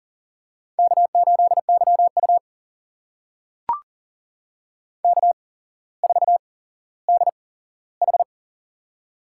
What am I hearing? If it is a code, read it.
Morse code, heard as a keyed tone, about 41 wpm, spaced as for 17 wpm, 700 Hz. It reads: K8YU K4DH
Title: USA Call Signs - 4 Characters - 8x Character Spacing - 40wpm